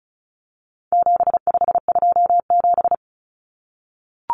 Morse code, heard as a keyed tone, about 34 wpm, 700 Hz. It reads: 7527
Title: Numbers - 4-Digit 35wpm